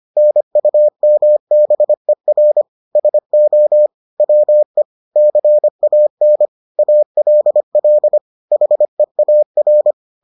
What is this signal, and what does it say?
Morse code, heard as a keyed tone, about 25 wpm, 600 Hz. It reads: NUMBER SO WE CAN ALL HEAR